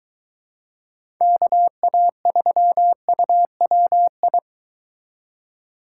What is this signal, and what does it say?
Morse code, heard as a keyed tone, about 23 wpm, 700 Hz. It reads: KA3UWI